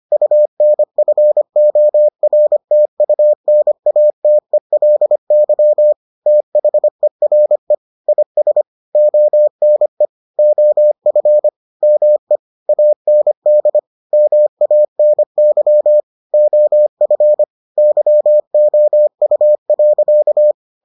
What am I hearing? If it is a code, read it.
Morse code, heard as a keyed tone, about 25 wpm, 600 Hz. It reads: UNFORTUNATELY THERE IS ONE OF ME AND MANY OF YOU.